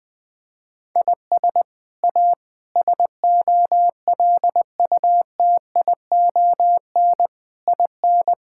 Morse code, heard as a keyed tone, about 20 wpm, 700 Hz. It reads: IS A SOLUTION IN
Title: Sets of 8 Words - 20wpm